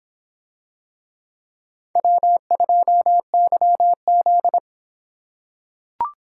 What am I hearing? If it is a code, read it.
Morse code, heard as a keyed tone, about 24 wpm, 700 Hz. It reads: W2YZ